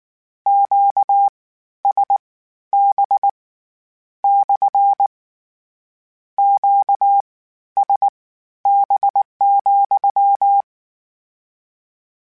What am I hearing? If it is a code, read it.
Morse code, heard as a keyed tone, about 19 wpm, spaced as for 12 wpm, 800 Hz. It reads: QSB / QSB,
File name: Q Calls / Q Calls-mp3s / Q CALLS SPSL @18